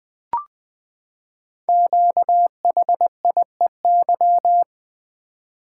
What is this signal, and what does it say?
Morse code, heard as a keyed tone, about 20 wpm, 700 Hz. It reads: QHIEY
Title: Code Groups - 5 Letters - 20wpm